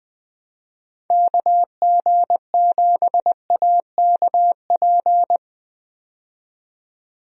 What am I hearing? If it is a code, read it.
Morse code, heard as a keyed tone, about 21 wpm, 700 Hz. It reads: KG7AKP
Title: USA Call Signs - 6 Characters - 20wpm